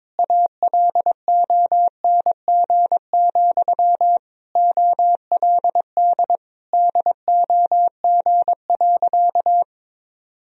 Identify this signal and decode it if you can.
Morse code, heard as a keyed tone, about 22 wpm, 700 Hz. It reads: ALONG, OLD DOG.
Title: Sentences from Top 1000 Words - Review Part 2 - 22wpm